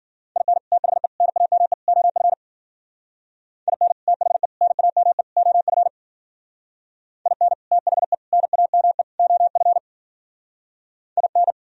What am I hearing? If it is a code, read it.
Morse code, heard as a keyed tone, about 51 wpm, 700 Hz. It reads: IN THE NAME OF IN THE NAME OF IN THE NAME OF IN